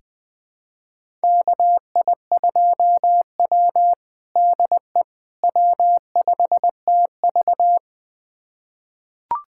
Morse code, heard as a keyed tone, about 20 wpm, 700 Hz. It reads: KI2W DE W5TV
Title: USA Callsign DE Callsign - 4 Characters - 20wpm